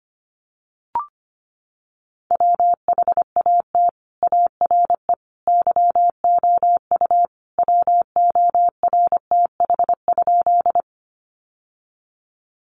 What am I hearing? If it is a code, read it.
Morse code, heard as a keyed tone, about 25 wpm, 700 Hz. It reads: WHAT ARE YOU WORTH?
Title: Sentences from Top 800 Words 25wpm